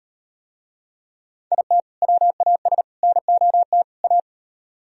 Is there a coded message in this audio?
Yes: Morse code from 1.5 to 4.2 s, heard as a keyed tone, about 38 wpm, 700 Hz. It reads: IT WAS NOT A